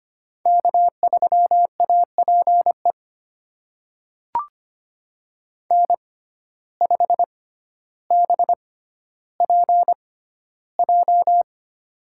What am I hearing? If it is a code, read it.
Morse code, heard as a keyed tone, about 25 wpm, spaced as for 13 wpm, 700 Hz. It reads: K3APE N5BPJ
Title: USA Call Signs - 5 Characters - 6x Character Spacing - 25wpm